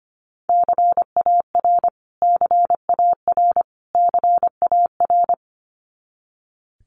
Morse code, heard as a keyed tone, about 25 wpm, 700 Hz. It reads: CAR CAR CAR
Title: Top 300 Words Repeated 25wpm